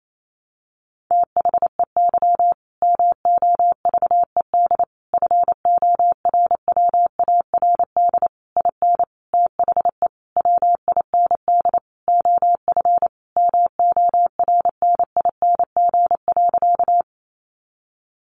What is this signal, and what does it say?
Morse code, heard as a keyed tone, about 28 wpm, 700 Hz. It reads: THEY MOVED FORWARD IN THE WIND OF MORNING.
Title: A Sound Of Thunder 28wpm